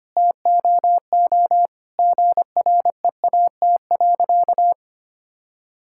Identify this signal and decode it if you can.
Morse code, heard as a keyed tone, about 25 wpm, 700 Hz. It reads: TOO GREAT.
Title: Sentences from Top 1000 Words - Review Part 6 - 25wpm